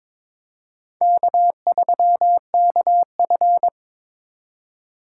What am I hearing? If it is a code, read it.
Morse code, heard as a keyed tone, about 22 wpm, 700 Hz. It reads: K3KF
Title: USA Call Signs - 4 Characters - 4x Character Spacing - 22wpm